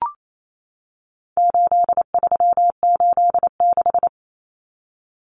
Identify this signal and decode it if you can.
Morse code, heard as a keyed tone, about 28 wpm, 700 Hz. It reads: 8386
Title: Numbers - 4-Digit 28wpm